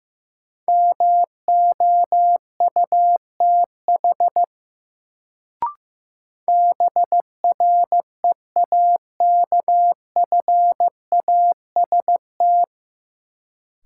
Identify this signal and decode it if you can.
Morse code, heard as a keyed tone, about 15 wpm, 700 Hz. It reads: MOUTH BREAKFAST
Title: Sentences from Top 1500 Words - New Word Drill Alt - 15wpm